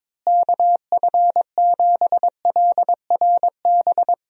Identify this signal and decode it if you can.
Morse code, heard as a keyed tone, about 22 wpm, 700 Hz. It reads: KF7LRB
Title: USA Call Signs - 6 Characters - Repeated 2x - 22wpm